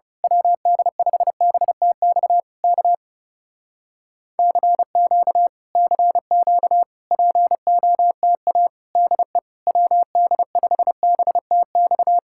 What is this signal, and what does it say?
Morse code, heard as a keyed tone, about 28 wpm, 700 Hz. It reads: WD5BTX K CQ CQ POTA DE WD5BTX
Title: Speed-Racing Common QSO Phrases Encore - 30wpm